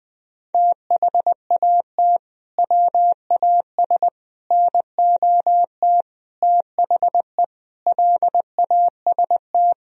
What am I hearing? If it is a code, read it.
Morse code, heard as a keyed tone, about 20 wpm, 700 Hz. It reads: THAT WAS NOT THE LAST
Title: Sentences from Top 1000 Words - Review Part 4 - 20wpm